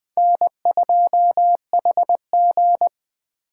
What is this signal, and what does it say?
Morse code, heard as a keyed tone, about 24 wpm, 700 Hz. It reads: N2HG
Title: Speed Racing - USA 1x2 Call Signs 20wpm